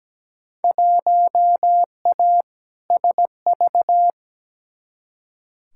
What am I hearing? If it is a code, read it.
Morse code, heard as a keyed tone, about 17 wpm, 700 Hz. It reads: 1A SV
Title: Field Day Exchanges - 17wpm